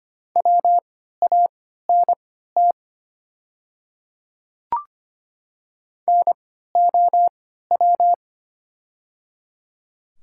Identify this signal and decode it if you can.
Morse code, heard as a keyed tone, about 25 wpm, spaced as for 14 wpm, 700 Hz. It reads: WANT NOW
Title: 3x Character Spacing - Top 300 Words - 25wpm